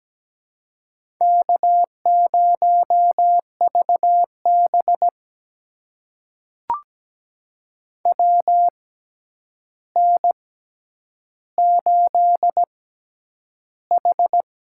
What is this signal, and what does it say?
Morse code, heard as a keyed tone, about 17 wpm, spaced as for 9 wpm, 700 Hz. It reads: K0VB WN8H
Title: USA Call Signs - 4 Characters - 6x Character Spacing - 17wpm